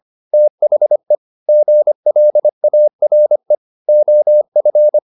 Morse code, heard as a keyed tone, about 25 wpm, 600 Hz. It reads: THE GLARE OF
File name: the old man and the sea part 1 25wpm